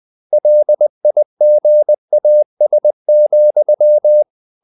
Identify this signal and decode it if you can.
Morse code, heard as a keyed tone, about 20 wpm, 600 Hz. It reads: LIGAS,
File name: the old man and the sea / the old man and the sea part 1 20wpm